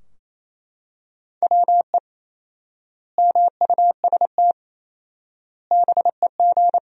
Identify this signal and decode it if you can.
Morse code, heard as a keyed tone, about 28 wpm, 700 Hz. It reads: WE MUST BEG
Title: 4x Word-Spacing - Sentences from Top 300 Words - 28wpm